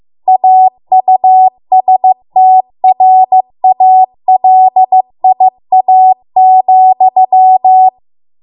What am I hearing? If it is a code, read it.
Morse code, heard as a keyed tone, about 15 wpm, 750 Hz. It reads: AUSTRALIA,